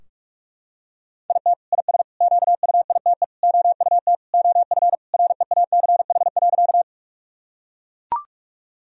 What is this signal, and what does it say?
Morse code, heard as a keyed tone, about 41 wpm, 700 Hz. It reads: IT IS QUITE OUT OF REACH.
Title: Sentences from Top 1000 Words - Review Part 2 - 45wpm